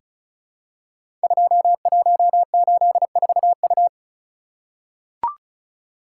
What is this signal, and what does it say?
Morse code, heard as a keyed tone, about 36 wpm, 700 Hz. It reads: 2184U
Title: Code Groups - 5 Letter:Numbers - 35wpm